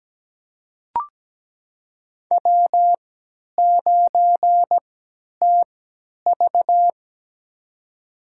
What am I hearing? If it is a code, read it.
Morse code, heard as a keyed tone, about 17 wpm, spaced as for 10 wpm, 700 Hz. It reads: W9TV
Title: USA Call Signs - 4 Characters - 3x Character Spacing - 17wpm